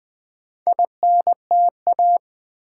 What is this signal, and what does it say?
Morse code, heard as a keyed tone, about 20 wpm, 690 Hz. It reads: INTA
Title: Build A Word - 2 Letters At A Time - 6-Letter Words - 20wpm